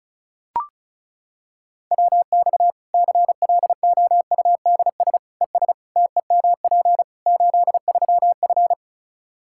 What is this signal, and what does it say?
Morse code, heard as a keyed tone, about 34 wpm, 700 Hz. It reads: WX CLOUDS ES TEMP 83F
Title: Practice QSO - Exchange 2 - Info - 35wpm